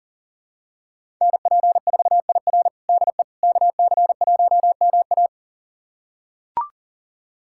Morse code, heard as a keyed tone, about 37 wpm, 700 Hz. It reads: NP4IR DE KC1MA
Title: USA Callsign DE Callsign - 5 Characters - 40wpm